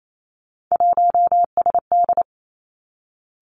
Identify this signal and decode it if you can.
Morse code, heard as a keyed tone, about 28 wpm, 700 Hz. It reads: 1SD
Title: Three Challenging Letter-Number Combinations 28wpm